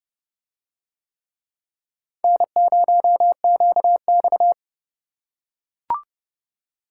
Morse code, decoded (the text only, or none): N0QX